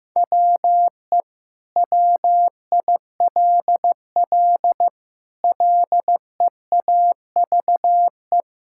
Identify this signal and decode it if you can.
Morse code, heard as a keyed tone, about 15 wpm, 700 Hz. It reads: WE WILL LEAVE
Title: Sentences from Top 1000 Words - Review Part 6 - 15wpm